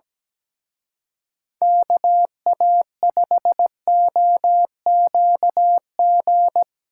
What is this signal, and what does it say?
Morse code, heard as a keyed tone, about 19 wpm, 700 Hz. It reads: KA5OQG
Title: Speed Racing - USA Call Signs 17wpm